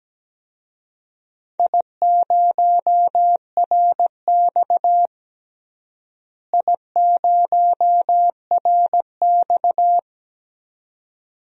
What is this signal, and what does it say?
Morse code, heard as a keyed tone, about 17 wpm, 700 Hz. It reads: I0RX I0RX
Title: DXCC-100 Call Signs - 4 Characters - Repeated 2x - 17wpm